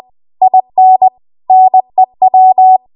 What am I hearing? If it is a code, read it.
Morse code, heard as a keyed tone, about 20 wpm, 750 Hz. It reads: IN NEW
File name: Archive / 20 WPM / 20WPM